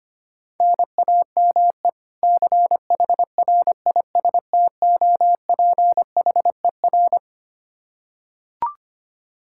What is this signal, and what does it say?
Morse code, heard as a keyed tone, about 25 wpm, 700 Hz. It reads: NAME CHRISTOPHER
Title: Practice QSO - Exchange 1 - Beginning - 25wpm